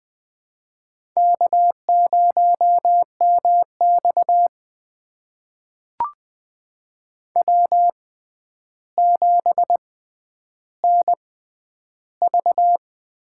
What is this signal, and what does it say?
Morse code, heard as a keyed tone, about 20 wpm, spaced as for 10 wpm, 700 Hz. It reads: K0MX W7NV
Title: USA Call Signs - 6x Character Spacing - 20wpm